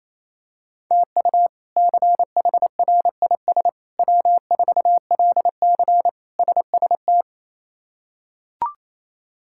Morse code, heard as a keyed tone, about 28 wpm, 700 Hz. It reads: TU CHRIS W4LC SST
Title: K1USN SST QSO Practice - 28wpm